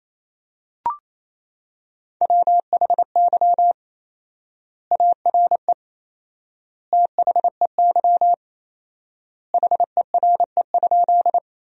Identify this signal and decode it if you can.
Morse code, heard as a keyed tone, about 28 wpm, 700 Hz. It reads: WHY ARE THEY HERE?
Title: 4x Word-Spacing - Sentences from Top 200 Words - 28wpm